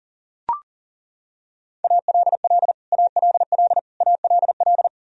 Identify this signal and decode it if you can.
Morse code, heard as a keyed tone, about 40 wpm, 700 Hz. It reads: ALL ALL ALL
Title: Top 100 Words Repeated 40wpm